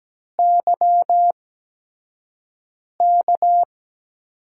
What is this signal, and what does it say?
Morse code, heard as a keyed tone, about 17 wpm, 700 Hz. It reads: Y K
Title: USA Call Signs - 8x Character Spacing - 17wpm